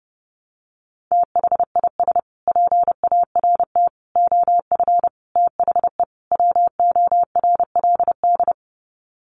Morse code, heard as a keyed tone, about 30 wpm, 700 Hz. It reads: THIS PART OF THE WORLD
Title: Sets of 5 Words Encore2 30wpm